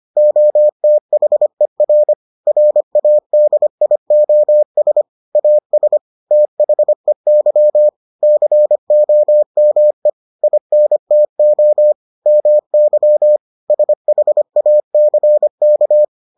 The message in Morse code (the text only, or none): OTHER RADIOS AS THEY COME INTO MY SHACK